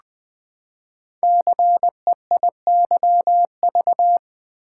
Morse code, heard as a keyed tone, about 20 wpm, 700 Hz. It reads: CEIYV
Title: Code Groups - 5 Letter:Numbers - 20wpm